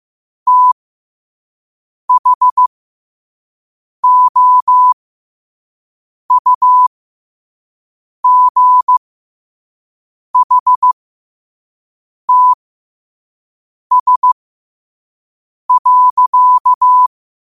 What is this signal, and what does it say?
Morse code, heard as a keyed tone, about 15 wpm, spaced as for 5 wpm, 1000 Hz. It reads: THOUGHTS.
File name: qotdfiles / 5wpm